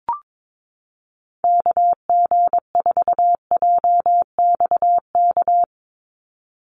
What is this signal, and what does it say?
Morse code, heard as a keyed tone, about 22 wpm, 700 Hz. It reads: KG4JXK
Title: USA Call Signs 22wpm